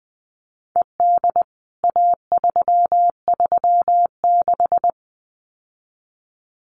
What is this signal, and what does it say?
Morse code, heard as a keyed tone, about 20 wpm, 700 Hz. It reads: ED A336
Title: CWT Exchange 20wpm